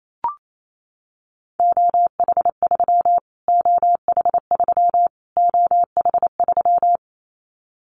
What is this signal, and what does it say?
Morse code, heard as a keyed tone, about 28 wpm, 700 Hz. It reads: OH3 OH3 OH3
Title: Callsign Prefixes for Top 25 DXCC - Repeated 3x - 28wpm